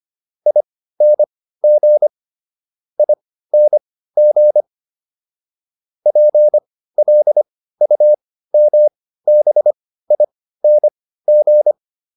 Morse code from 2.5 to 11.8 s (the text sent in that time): ING PLUMBING